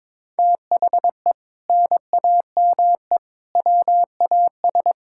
Morse code, heard as a keyed tone, about 22 wpm, 700 Hz. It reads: THE NAME WAS
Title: Sentences from Top 1600 Words 22wpm